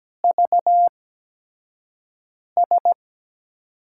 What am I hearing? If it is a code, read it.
Morse code, heard as a keyed tone, about 17 wpm, spaced as for 5 wpm, 700 Hz. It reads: VS
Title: USA Call Signs - 4 Characters - 8x Character Spacing - 17wpm